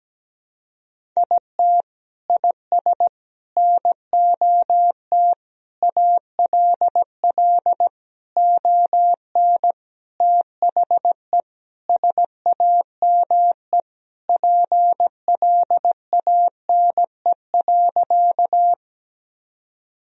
Morse code, heard as a keyed tone, about 17 wpm, 700 Hz. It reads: IT IS NOT ALL ON THE SAME PLANE.
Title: Sentences from Top 1000 Words - Review Part 3 - 17wpm